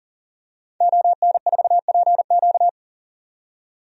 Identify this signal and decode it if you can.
Morse code, heard as a keyed tone, about 40 wpm, 700 Hz. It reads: ON4PQ
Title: DXCC-100 Call Signs - 5 Characters - 40wpm